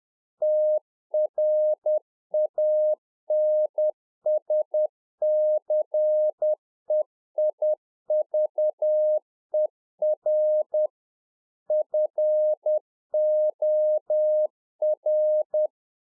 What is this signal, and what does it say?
Morse code, heard as a keyed tone, about 10 wpm, 610 Hz. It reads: TRANSCEIVER FOR